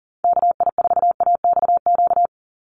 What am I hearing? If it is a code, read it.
Morse code, heard as a keyed tone, about 40 wpm, 700 Hz. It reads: KI4AXQ
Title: USA Call Signs 40wpm